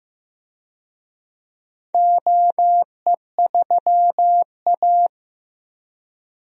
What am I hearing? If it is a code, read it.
Morse code, heard as a keyed tone, about 15 wpm, 700 Hz. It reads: OE3A
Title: DXCC-100 Call Signs - 4 Characters - 15wpm